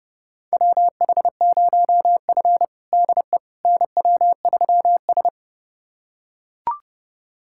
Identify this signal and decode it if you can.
Morse code, heard as a keyed tone, about 30 wpm, 700 Hz. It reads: WH0F DE NW3S
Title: USA Callsign DE Callsign - 4 Characters - 30wpm